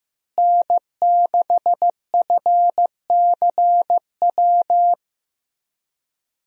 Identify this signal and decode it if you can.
Morse code, heard as a keyed tone, about 15 wpm, 700 Hz. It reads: N6FCW
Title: USA Call Signs - 5 Characters - Repeated 3x - 15wpm